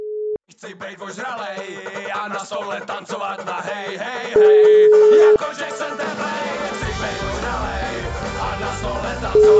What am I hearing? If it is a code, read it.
Ringback tone (Europe, 425 Hz), with speech.